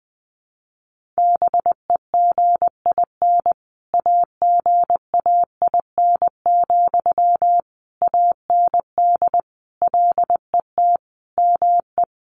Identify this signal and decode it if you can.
Morse code, heard as a keyed tone, about 20 wpm, 700 Hz. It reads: BEGIN AGAIN, AND LET ME
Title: Sentences from Top 800 Words 20wpm